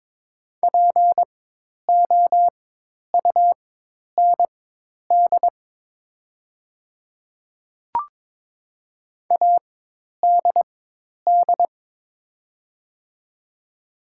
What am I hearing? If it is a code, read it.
Morse code, heard as a keyed tone, about 22 wpm, spaced as for 10 wpm, 700 Hz. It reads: POUND ADD
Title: 4x Character Spacing - Top 500 Words - 22wpm